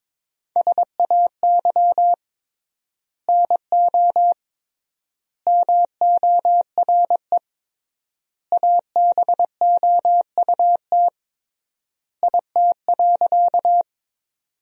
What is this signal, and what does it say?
Morse code, heard as a keyed tone, about 21 wpm, 700 Hz. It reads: SAY NO MORE ABOUT IT.
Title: 3x Word-Spacing - Sentences from Top 200 Words - 22wpm